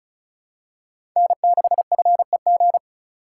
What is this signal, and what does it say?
Morse code, heard as a keyed tone, about 36 wpm, 700 Hz. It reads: N6FEG
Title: USA Call Signs - 5 Characters - 3x Character Spacing - 35wpm